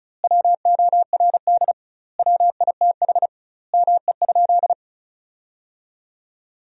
Morse code, heard as a keyed tone, about 34 wpm, 700 Hz. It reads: WORD WITH ME?
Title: 2x Word-Spacing - Sentences from Top 300 Words - 35wpm